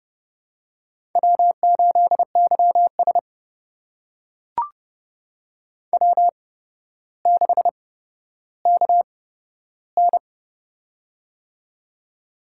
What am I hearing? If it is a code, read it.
Morse code, heard as a keyed tone, about 30 wpm, spaced as for 13 wpm, 700 Hz. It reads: W8YS W6KN
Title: USA Call Signs - 4 Characters - 8x Character Spacing - 30wpm